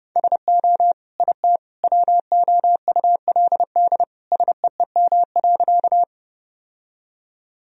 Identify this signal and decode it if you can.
Morse code, heard as a keyed tone, about 30 wpm, 700 Hz. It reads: SO IT WOULD SEEM.
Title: Sentences from Top 1000 Words - Review Part 1 - 30wpm